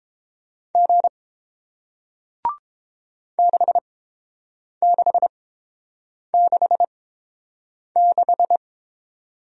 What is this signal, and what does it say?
Morse code, heard as a keyed tone, about 30 wpm, spaced as for 7 wpm, 700 Hz. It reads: G 6666